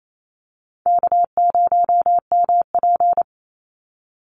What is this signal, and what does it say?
Morse code, heard as a keyed tone, about 28 wpm, 700 Hz. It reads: K0MP
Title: Call Signs Repeated 3x - 28wpm